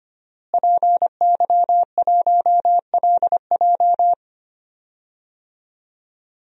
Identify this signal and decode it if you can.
Morse code, heard as a keyed tone, about 25 wpm, 700 Hz. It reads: PY1LJ
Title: DXCC-100 Call Signs - 5 Characters - Repeated 2x - 25wpm